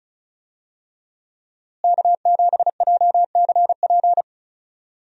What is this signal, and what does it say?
Morse code, heard as a keyed tone, about 36 wpm, 700 Hz. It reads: K7JCP
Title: USA Call Signs - 5 Characters - Repeated 3x - 35wpm